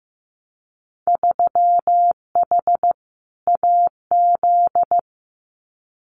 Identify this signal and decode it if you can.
Morse code, heard as a keyed tone, about 15 wpm, 700 Hz. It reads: 3H AZ
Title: Winter Field Day Exchanges Encore 15wpm